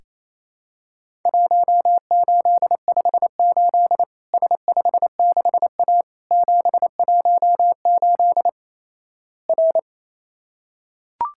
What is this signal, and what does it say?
Morse code, heard as a keyed tone, about 28 wpm, 700 Hz. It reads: 1858 S56A 718 R
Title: Worked All Europe DX Contest - QTC Exchange - 28wpm